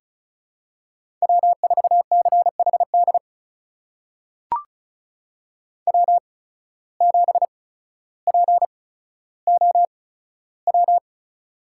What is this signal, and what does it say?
Morse code, heard as a keyed tone, about 36 wpm, spaced as for 15 wpm, 700 Hz. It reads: W4CHD W7POW
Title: USA Call Signs - 5 Characters - 8x Character Spacing - 35wpm